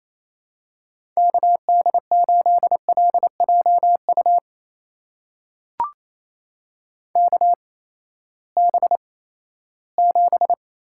Morse code, heard as a keyed tone, about 27 wpm, spaced as for 16 wpm, 700 Hz. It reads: KD8LJU KB7
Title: USA Call Signs - 8x Character Spacing - 28wpm